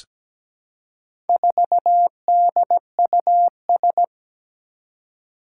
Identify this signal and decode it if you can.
Morse code, heard as a keyed tone, about 17 wpm, 700 Hz. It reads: 4DUS
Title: Four Challenging Letter-Number Combinations 17wpm